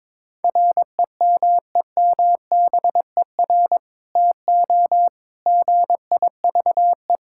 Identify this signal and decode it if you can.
Morse code, heard as a keyed tone, about 21 wpm, 700 Hz. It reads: REMEMBER TO GIVE
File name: practice / 22wpm / Sentences with 2-Word Phrases - 22wpm